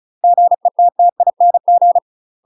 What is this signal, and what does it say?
Morse code, heard as a keyed tone, about 34 wpm, 700 Hz. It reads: GETTING
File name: UK MASTER QSOs TRAINING / 35WPM / UK MASTER QSOs 0039